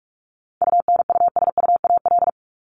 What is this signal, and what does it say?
Morse code, heard as a keyed tone, about 44 wpm, 700 Hz. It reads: UNUSUAL
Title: Top 1251-1500 Words 45wpm